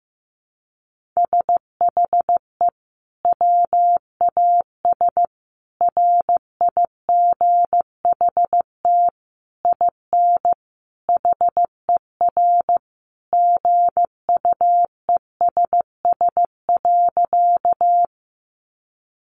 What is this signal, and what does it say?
Morse code, heard as a keyed tone, about 15 wpm, 700 Hz. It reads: SHE WAS RIGHT IN HER GUESS.
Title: Sentences from Top 1500 Words 15wpm